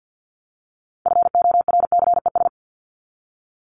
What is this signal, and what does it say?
Morse code, heard as a keyed tone, about 50 wpm, 700 Hz. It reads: FORCES
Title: ES:S Suffix - Top 100 Words with Base Words 50wpm